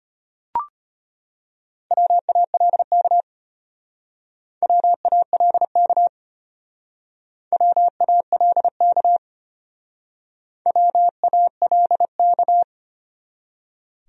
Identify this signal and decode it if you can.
Morse code, heard as a keyed tone, about 31 wpm, 700 Hz. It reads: WALK WALK WALK WALK
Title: Speed-Racing Top 300 Words 25wpm